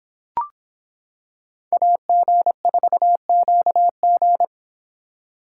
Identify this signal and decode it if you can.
Morse code, heard as a keyed tone, about 26 wpm, 700 Hz. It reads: AG4QG